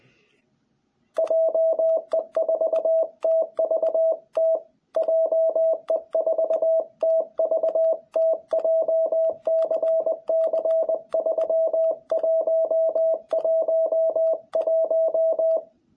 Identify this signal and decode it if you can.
Morse code, heard as a keyed tone, about 20 wpm, 660 Hz. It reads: JE4TVT JE4TVT JCC3111
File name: jh8cbh / 2021cw